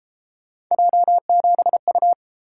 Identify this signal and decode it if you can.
Morse code, heard as a keyed tone, about 31 wpm, 700 Hz. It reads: J7U